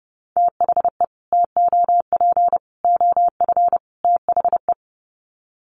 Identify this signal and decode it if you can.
Morse code, heard as a keyed tone, about 28 wpm, 700 Hz. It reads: THE TOP OF THE
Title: Sets of 4 Words 30wpm